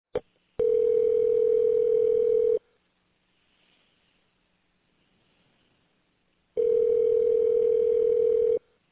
Ringback tone (North America), 440 + 480 Hz.